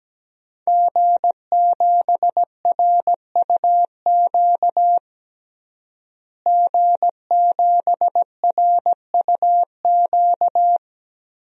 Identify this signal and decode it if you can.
Morse code, heard as a keyed tone, about 17 wpm, 700 Hz. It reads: G7RUQ G7RUQ
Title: DXCC-100 Call Signs - 5 Characters - Repeated 2x - 17wpm